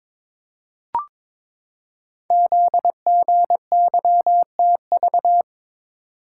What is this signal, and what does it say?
Morse code, heard as a keyed tone, about 22 wpm, 700 Hz. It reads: ZGYTV
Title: Code Groups - 5 Letters - 22wpm